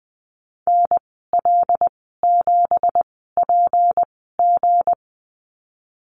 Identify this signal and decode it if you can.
Morse code, heard as a keyed tone, about 20 wpm, spaced as for 15 wpm, 700 Hz. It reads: NL7PG
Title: 173 Lesson 28g - Number 9 - Call Signs 20wpm